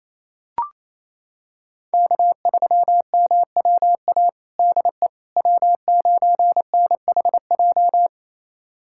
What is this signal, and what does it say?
Morse code, heard as a keyed tone, about 28 wpm, 700 Hz. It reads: K3MWA DE W9NHJ
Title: USA Callsign DE Callsign - 5 Characters - 28wpm